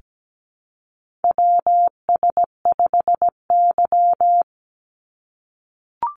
Morse code, heard as a keyed tone, about 17 wpm, 700 Hz. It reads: WS5Y
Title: USA Call Signs 17wpm